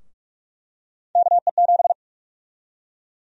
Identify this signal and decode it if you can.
Morse code, heard as a keyed tone, about 46 wpm, 700 Hz. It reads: KE7
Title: USA Call Sign Prefixes - Repeated 2x - 45wpm